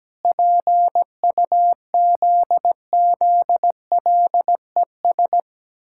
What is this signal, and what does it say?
Morse code, heard as a keyed tone, about 17 wpm, 700 Hz. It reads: PUZZLES
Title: Pangram Challenge v2 - 17wpm